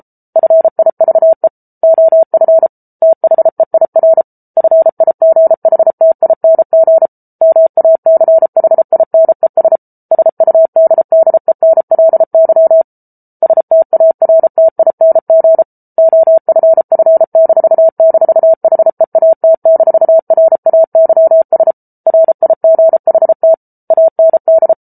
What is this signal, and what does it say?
Morse code, heard as a keyed tone, about 34 wpm, 650 Hz. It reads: FIVE OF THEIR FIGHTING MACHINES SUDDENLY STARTING OFF--HEAT-RAYS RIGHT AND